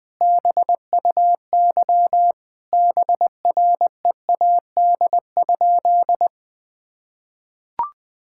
Morse code, heard as a keyed tone, about 21 wpm, 700 Hz. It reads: BUY BREAD?
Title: Sentences from Top 1600 Words 20wpm